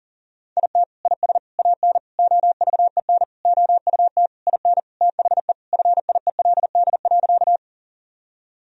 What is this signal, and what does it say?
Morse code, heard as a keyed tone, about 37 wpm, 700 Hz. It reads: IT IS AN OVEN OUT IN THE FIELD.
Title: Sentences from Top 1000 Words - Review Part 4 - 40wpm